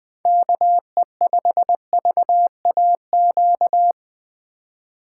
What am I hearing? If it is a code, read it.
Morse code, heard as a keyed tone, about 20 wpm, 700 Hz. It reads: KE5VAQ
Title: USA Call Signs - 6 Characters - 2x Character Spacing - 20wpm